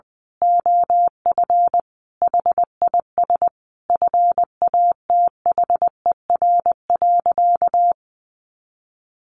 Morse code, heard as a keyed tone, about 20 wpm, 700 Hz. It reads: OF HIS FATHER.
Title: Sentences from Top 1500 Words 20wpm